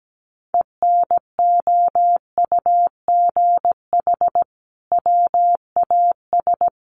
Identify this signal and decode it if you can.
Morse code, heard as a keyed tone, about 17 wpm, 700 Hz. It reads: ENOUGH WAS
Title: Sentences from Top 1400 Words 17wpm